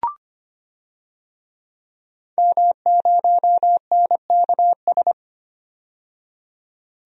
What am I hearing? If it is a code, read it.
Morse code, heard as a keyed tone, about 25 wpm, 700 Hz. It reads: M0NKS
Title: DXCC-100 Call Signs - 5 Characters - 25wpm